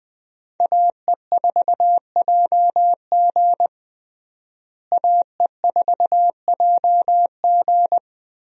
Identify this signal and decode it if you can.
Morse code, heard as a keyed tone, about 21 wpm, 700 Hz. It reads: AE4JG AE4JG
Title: USA Call Signs - 5 Characters - Repeated 3x - 20wpm